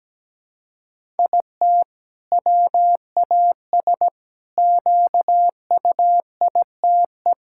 Morse code, heard as a keyed tone, about 17 wpm, 700 Hz. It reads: IT WAS QUITE
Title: Sentences from Top 1000 Words - Review Part 5 - 17wpm